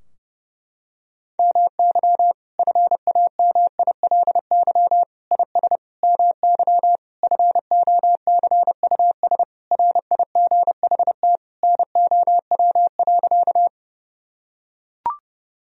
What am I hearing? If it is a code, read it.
Morse code, heard as a keyed tone, about 32 wpm, 700 Hz. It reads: MY FAMILY IS MY FOCUS RIGHT NOW.
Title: Sentences from Top 1000 Words - Review Part 4 - 30wpm